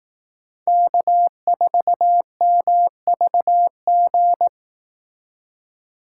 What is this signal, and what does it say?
Morse code, heard as a keyed tone, about 18 wpm, 700 Hz. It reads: K4MVG
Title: Speed Racing - USA Call Signs 15wpm